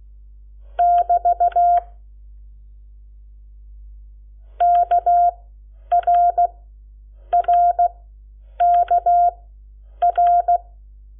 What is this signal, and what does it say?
Morse code, heard as a keyed tone, about 16 wpm, spaced as for 8 wpm, 700 Hz. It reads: <BT> KRRKR
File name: Lecciones / CLASE Nº7